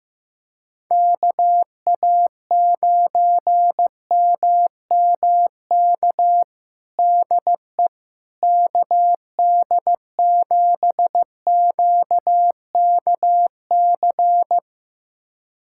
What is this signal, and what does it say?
Morse code, heard as a keyed tone, about 15 wpm, 700 Hz. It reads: KA9MMK DE KD7QKC
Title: USA Callsign DE Callsign 15wpm